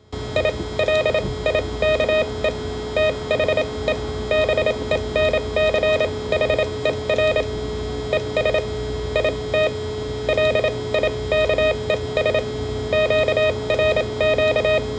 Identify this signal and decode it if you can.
Morse code, heard as a keyed tone, about 27 wpm, spaced as for 22 wpm, 620 Hz. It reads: I LIKE THE BENCHER ES IT LIKES QRQ
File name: qrq4